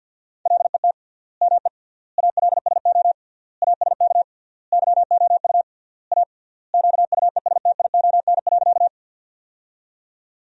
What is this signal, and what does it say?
Morse code, heard as a keyed tone, about 47 wpm, 700 Hz. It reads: LET ME ALSO ASK YOU A QUESTION.
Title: 3x Word-Spacing - Sentences from Top 400 Words - 50wpm